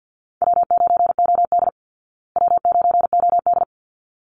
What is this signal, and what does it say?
Morse code, heard as a keyed tone, about 50 wpm, 700 Hz. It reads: W9OD W9OD
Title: Call Signs Repeated 3x - 50wpm